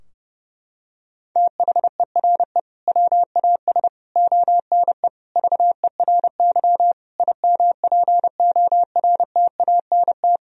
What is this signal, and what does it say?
Morse code, heard as a keyed tone, about 30 wpm, 700 Hz. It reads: THERE WAS ONE VERY IMPORTANT
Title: Sentences from Top 1000 Words - Review Part 3 - 30wpm